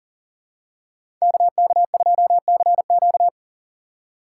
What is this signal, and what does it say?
Morse code, heard as a keyed tone, about 40 wpm, 700 Hz. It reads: KK2CQ
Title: USA Call Signs - 5 Characters - 3x Character Spacing - 40wpm